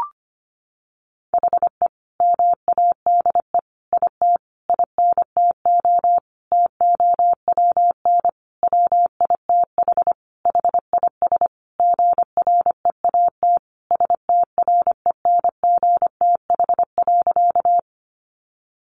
Morse code, heard as a keyed tone, about 25 wpm, 700 Hz. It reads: HE MADE IT INTO TOWN WITH HIS GREAT STRENGTH.
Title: Sentences from Top 1300 Words 25wpm